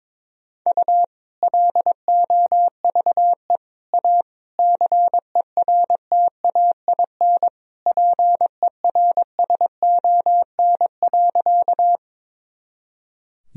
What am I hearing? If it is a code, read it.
Morse code, heard as a keyed tone, about 22 wpm, 700 Hz. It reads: U LOVE A CERTAIN PERSON.
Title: Sentences from Top 500 Words and Abbreviations 22wpm